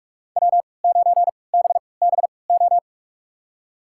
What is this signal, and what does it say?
Morse code, heard as a keyed tone, about 44 wpm, spaced as for 25 wpm, 700 Hz. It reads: W9BBO
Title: USA Call Signs - 3x Character Spacing - 45wpm